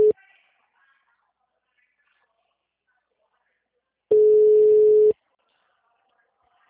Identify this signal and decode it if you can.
Ringback tone (Europe, 425 Hz), with speech.